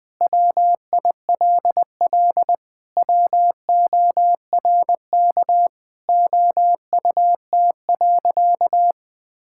Morse code, heard as a keyed tone, about 20 wpm, 700 Hz. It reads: WILL WORK OUT.
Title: Sentences with 3-Word Phrases - Spoken Beginning - 20wpm